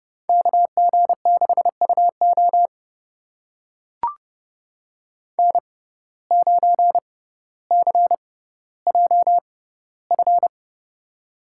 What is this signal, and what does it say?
Morse code, heard as a keyed tone, about 32 wpm, spaced as for 16 wpm, 700 Hz. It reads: KG6UO N9CJF
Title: USA Call Signs - 5 Characters - 6x Character Spacing - 30wpm